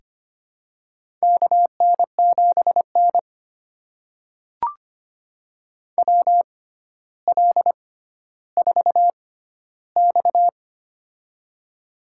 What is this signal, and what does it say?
Morse code, heard as a keyed tone, about 25 wpm, spaced as for 13 wpm, 700 Hz. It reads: KN7N WL4X
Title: USA Call Signs - 4 Characters - 6x Character Spacing - 25wpm